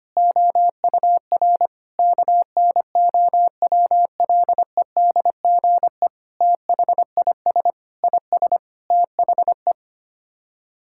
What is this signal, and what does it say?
Morse code, heard as a keyed tone, about 25 wpm, 700 Hz. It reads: OUR KNOWLEDGE THIS IS THE
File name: Sets of 9 Words - 25wpm